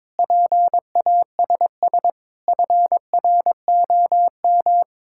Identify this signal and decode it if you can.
Morse code, heard as a keyed tone, about 22 wpm, 700 Hz. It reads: PASS FROM
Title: Sentences from Top 1000 Words - Review Part 6 - 22wpm